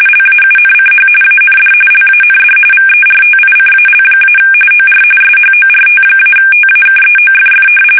RTTY, about 45.45 baud. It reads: LAUNCH AND LAND VERTICAL- OR SHORT-TAKEOFF
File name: fsk850 / battleshipnj